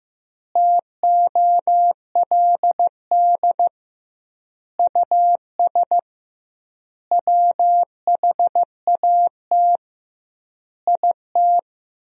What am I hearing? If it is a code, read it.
Morse code, heard as a keyed tone, about 15 wpm, 700 Hz. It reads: TOLD US WHAT IT
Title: 2x Word-Spacing - Sentences from Top 400 Words - 15wpm